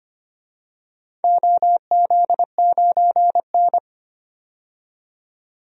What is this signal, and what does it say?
Morse code, heard as a keyed tone, about 25 wpm, 700 Hz. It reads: OZ9N
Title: DXCC-100 Call Signs - 4 Characters - 25wpm